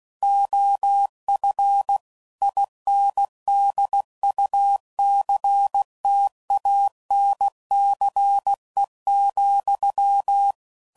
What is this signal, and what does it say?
Morse code, heard as a keyed tone, about 16 wpm, 800 Hz. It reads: OF INDUCTANCE,
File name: CW Code Groups @15WPM / Technical Subjects / XL